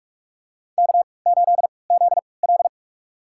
Morse code, heard as a keyed tone, about 46 wpm, spaced as for 25 wpm, 700 Hz. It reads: K8ZL
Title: USA Call Signs - 4 Characters - 3x Character Spacing - 45wpm